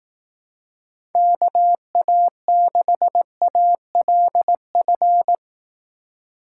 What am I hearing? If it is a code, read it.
Morse code, heard as a keyed tone, about 18 wpm, 700 Hz. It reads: KA6ALF
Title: Speed Racing - USA 2x3 Call Signs 15wpm